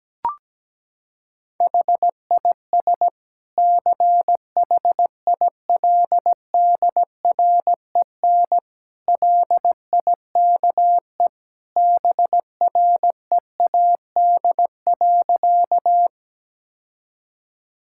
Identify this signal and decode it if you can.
Morse code, heard as a keyed tone, about 17 wpm, 700 Hz. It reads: HIS CHILDREN LIKE BREAD.
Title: Sentences from Top 1600 Words 17wpm